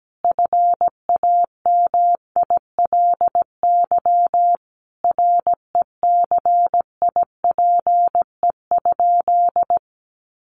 Morse code, heard as a keyed tone, about 17 wpm, 700 Hz. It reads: FAMILY RECIPE?
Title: Sentences from Top 1200 Words 17wpm